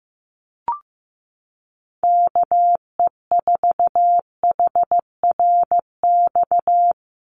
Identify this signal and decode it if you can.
Morse code, heard as a keyed tone, about 15 wpm, 700 Hz. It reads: KE4HRX
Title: USA Call Signs 15wpm